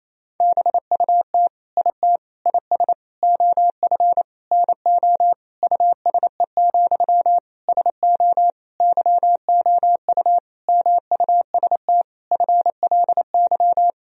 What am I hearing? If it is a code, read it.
Morse code, heard as a keyed tone, about 28 wpm, 700 Hz. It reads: BUT IT IS OF NO USE, SO YOU MUST FLY
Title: Sentences from Top 1000 Words - Review Part 2 - 28wpm